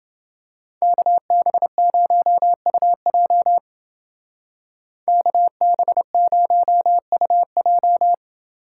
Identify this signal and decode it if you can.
Morse code, heard as a keyed tone, about 28 wpm, 700 Hz. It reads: KB0UJ KB0UJ